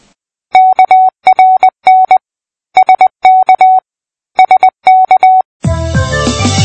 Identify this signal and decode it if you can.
Morse code, heard as a keyed tone, about 20 wpm, 750 Hz. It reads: KRN SK SK
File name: pi4aa / PI4AA Uitzending